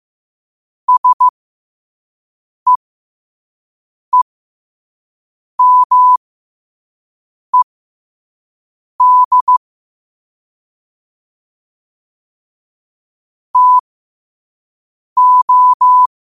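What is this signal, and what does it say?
Morse code, heard as a keyed tone, about 15 wpm, spaced as for 5 wpm, 1000 Hz. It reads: SEEMED TO
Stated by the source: Quote-of-the-day Morse code practice at 5 wpm.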